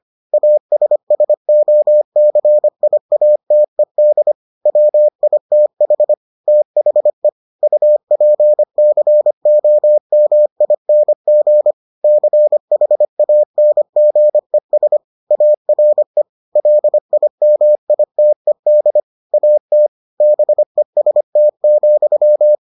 Morse code, heard as a keyed tone, about 25 wpm, 600 Hz. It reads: ASSOCIATED WITH THE UPCOMING CHANGES ARE LIMITED AT BEST,